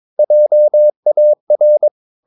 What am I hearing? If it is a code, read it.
Morse code, heard as a keyed tone, about 22 wpm, 600 Hz. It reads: JAR
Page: Romanen Never af Ken Follet på engelsk.